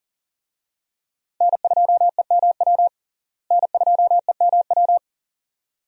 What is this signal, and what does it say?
Morse code, heard as a keyed tone, about 40 wpm, 700 Hz. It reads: N2EMW N2EMW
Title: USA Call Signs - 5 Characters - Repeated 2x - 40wpm